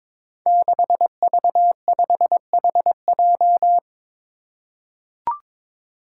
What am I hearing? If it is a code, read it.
Morse code, heard as a keyed tone, about 21 wpm, 700 Hz. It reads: 6V5HJ
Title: Five Challenging Letter-Number Combinations 22wpm